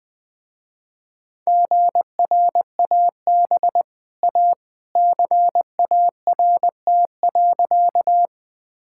Morse code, heard as a keyed tone, about 20 wpm, 700 Hz. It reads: GRAB A CART.
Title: Easy Sentences with Hints - 20wpm